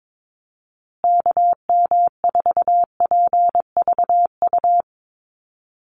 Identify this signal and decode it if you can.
Morse code, heard as a keyed tone, about 22 wpm, 700 Hz. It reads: KM4PVU
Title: USA Call Signs Repeated 3x 22wpm